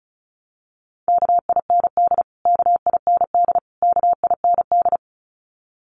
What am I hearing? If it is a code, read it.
Morse code, heard as a keyed tone, about 34 wpm, 700 Hz. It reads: KIND KIND KIND
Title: Top 200 Words Repeated 35wpm